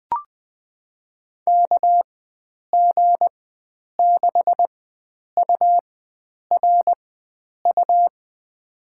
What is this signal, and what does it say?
Morse code, heard as a keyed tone, about 20 wpm, spaced as for 9 wpm, 700 Hz. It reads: KG6URU
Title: USA Call Signs - 4x Character Spacing - 20wpm